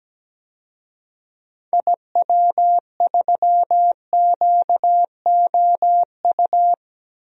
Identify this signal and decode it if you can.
Morse code, heard as a keyed tone, about 17 wpm, 700 Hz. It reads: IW3QOU
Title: DXCC-100 Call Signs - 6 Characters - Repeated 3x - 17wpm